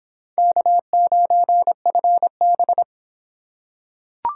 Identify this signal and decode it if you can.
Morse code, heard as a keyed tone, about 25 wpm, 700 Hz. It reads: K9FB